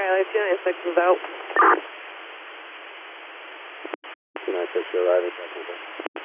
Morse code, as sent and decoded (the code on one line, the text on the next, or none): none